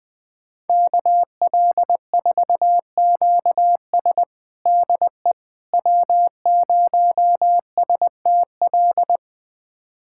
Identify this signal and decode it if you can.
Morse code, heard as a keyed tone, about 20 wpm, 700 Hz. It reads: KL4QS DE W0STL
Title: USA Callsign DE Callsign - 5 Characters - 20wpm